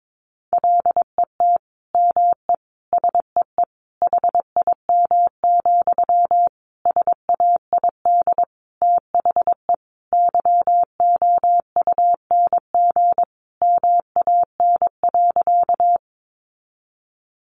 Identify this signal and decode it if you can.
Morse code, heard as a keyed tone, about 22 wpm, 700 Hz. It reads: LET ME SEE HIM, SAID THE YOUNG MAN.
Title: Sentences from Top 400 Words 22wpm